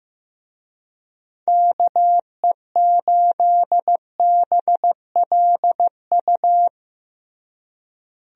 Morse code, heard as a keyed tone, about 15 wpm, 700 Hz. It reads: KE8BLU
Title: USA Call Signs - 6 Characters - Repeated 3x - 15wpm